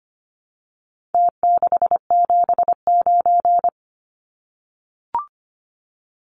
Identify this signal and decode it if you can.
Morse code, heard as a keyed tone, about 25 wpm, 700 Hz. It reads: T679
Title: Numbers - Contest Style - 4-Digit 25wpm